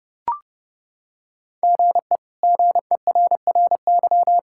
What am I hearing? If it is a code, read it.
Morse code, heard as a keyed tone, about 30 wpm, 700 Hz. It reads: GE GERRY
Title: K1USN SST QSO Practice - 30wpm